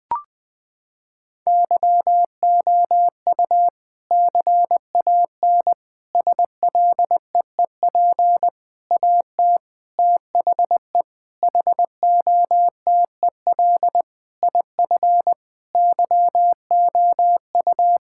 Morse code, heard as a keyed tone, about 20 wpm, 700 Hz. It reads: YOU CAN SLEEP AT THE HOTEL IF YOU
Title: Sentences from Top 1600 Words 20wpm